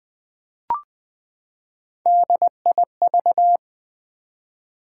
Morse code, heard as a keyed tone, about 21 wpm, 700 Hz. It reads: DIV
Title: Build A Word - 3 Letters At A Time - 9-Letter Words - 20wpm